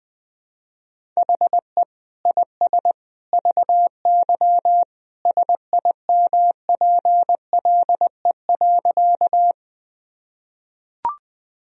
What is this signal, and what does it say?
Morse code, heard as a keyed tone, about 20 wpm, 700 Hz. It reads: HE IS VY SIMPLE.
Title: Sentences from Top 400 Words and Abbreviations 20wpm